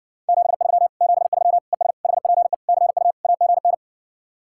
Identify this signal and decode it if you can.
Morse code, heard as a keyed tone, about 65 wpm, 700 Hz. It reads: 73 73 ES HPE CU AGN